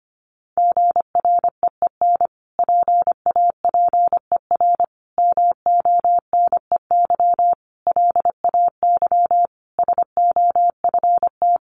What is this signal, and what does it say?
Morse code, heard as a keyed tone, about 25 wpm, 700 Hz. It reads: GREEN PAPER MONEY LAY SOFT
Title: Sentences from Top 1500 Words 25wpm